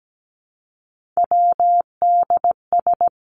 Morse code, heard as a keyed tone, about 17 wpm, 700 Hz. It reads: WDS
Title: Abbreviations 17wpm